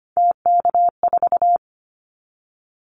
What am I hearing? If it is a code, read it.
Morse code, heard as a keyed tone, about 25 wpm, 700 Hz. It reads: TK4
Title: Callsign Prefixes for Top 100 DXCC - 25wpm